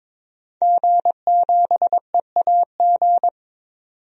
Morse code, heard as a keyed tone, about 22 wpm, 700 Hz. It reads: G7EAG
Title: DXCC-100 Call Signs - 5 Characters - 22wpm